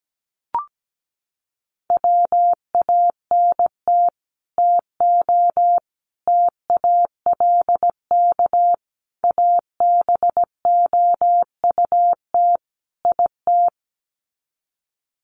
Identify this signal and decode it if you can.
Morse code, heard as a keyed tone, about 17 wpm, 700 Hz. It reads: WANT TO TALK ABOUT IT
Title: Sets of 5 Words Encore 17wpm